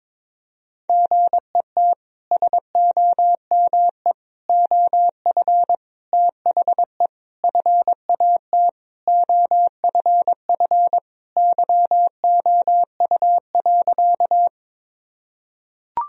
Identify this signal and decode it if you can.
Morse code, heard as a keyed tone, about 22 wpm, 700 Hz. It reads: GET SOME OF THE FAT OFF YOU.
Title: Sentences from Top 1000 Words - Review Part 4 - 22wpm